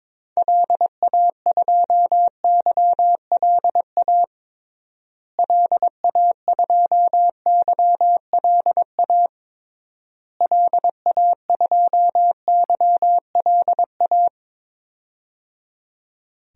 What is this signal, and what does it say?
Morse code, heard as a keyed tone, about 21 wpm, 700 Hz. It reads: LA2YLA LA2YLA LA2YLA
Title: DXCC-100 Call Signs - 6 Characters - Repeated 3x - 22wpm